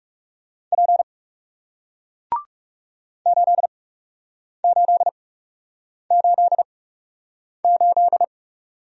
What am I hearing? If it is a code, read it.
Morse code, heard as a keyed tone, about 42 wpm, spaced as for 8 wpm, 700 Hz. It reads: P 8888